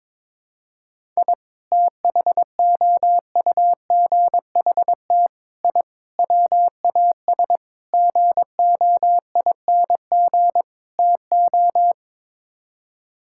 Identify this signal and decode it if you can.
Morse code, heard as a keyed tone, about 22 wpm, 700 Hz. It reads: I THOUGHT I WAS GOING TO
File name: practice / 22wpm / Sets of 6 Words - 22wpm